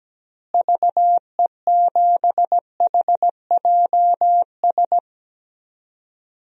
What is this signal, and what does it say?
Morse code, heard as a keyed tone, about 17 wpm, 700 Hz. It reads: VE7HJS
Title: DXCC-100 Call Signs - 6 Characters - Repeated 2x - 17wpm